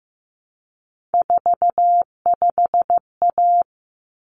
Morse code, heard as a keyed tone, about 15 wpm, 700 Hz. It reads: 45A
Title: Common Cut Numbers - 3-Digit 15wpm